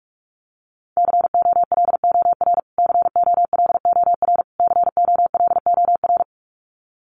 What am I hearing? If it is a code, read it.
Morse code, heard as a keyed tone, about 44 wpm, 700 Hz. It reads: COLOR COLOR COLOR
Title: Top 500 Words Repeated 45wpm